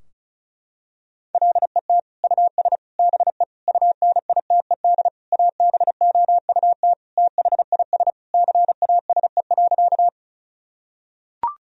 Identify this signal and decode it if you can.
Morse code, heard as a keyed tone, about 36 wpm, 700 Hz. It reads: LET US BE UNITED ABOUT THIS CASE.
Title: Sentences from Top 1000 Words - Review Part 4 - 35wpm